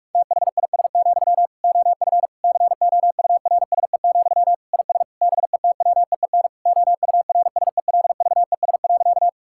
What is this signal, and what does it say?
Morse code, heard as a keyed tone, about 46 wpm, 700 Hz. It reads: THIS, OF COURSE, IS BETWEEN OURSELVES.
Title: Sentences from Top 1600 Words 45wpm